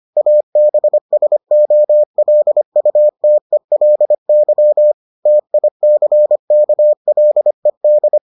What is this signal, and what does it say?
Morse code, heard as a keyed tone, about 25 wpm, 600 Hz. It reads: ABSOLUTELY TICKLED